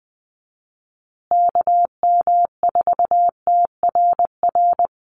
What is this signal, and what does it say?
Morse code, heard as a keyed tone, about 20 wpm, 700 Hz. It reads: KM4TRR
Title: USA Call Signs Repeated 2x 20wpm